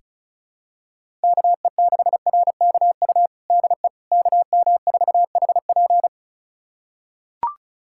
Morse code, heard as a keyed tone, about 34 wpm, 700 Hz. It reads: KE6RKU DE KM4HP
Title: USA Callsign DE Callsign 35wpm